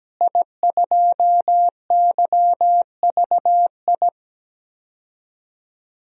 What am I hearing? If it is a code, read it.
Morse code, heard as a keyed tone, about 17 wpm, 700 Hz. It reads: I2YVI
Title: DXCC-100 Call Signs - 5 Characters - Repeated 3x - 17wpm